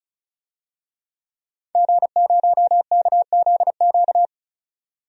Morse code, heard as a keyed tone, about 36 wpm, 700 Hz. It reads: G0KZQ
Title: DXCC-100 Call Signs - 5 Characters - 35wpm